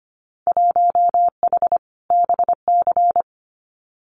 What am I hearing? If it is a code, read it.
Morse code, heard as a keyed tone, about 25 wpm, 700 Hz. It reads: 1H BC
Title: Winter Field Day Exchanges Encore 25wpm